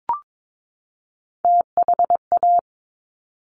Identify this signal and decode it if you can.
Morse code, heard as a keyed tone, about 22 wpm, 700 Hz. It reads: THA
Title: Top Three Letter Combinations in Words 22wpm